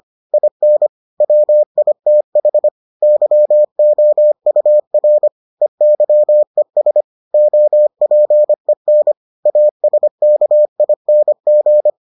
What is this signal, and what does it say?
Morse code, heard as a keyed tone, about 25 wpm, 600 Hz. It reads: IN WITH YOUR EYES OPEN ASKING